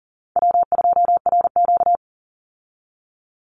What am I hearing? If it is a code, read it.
Morse code, heard as a keyed tone, about 40 wpm, 700 Hz. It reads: W2RQ
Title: Easy Call Signs Repeated 2x - 40wpm